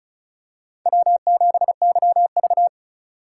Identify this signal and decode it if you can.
Morse code, heard as a keyed tone, about 36 wpm, 700 Hz. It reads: W7YV
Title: USA Call Signs - 4 Characters - 4x Character Spacing - 35wpm